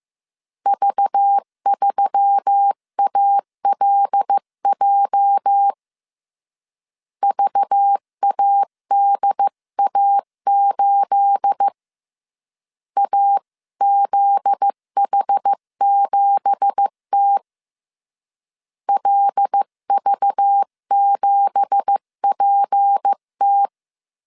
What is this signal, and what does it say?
Morse code, heard as a keyed tone, about 15 wpm, 800 Hz. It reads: V3ALJ VADA8 AZH7T LV7PT
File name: CW / DET / 1 9 1500C S B 14wpm 3302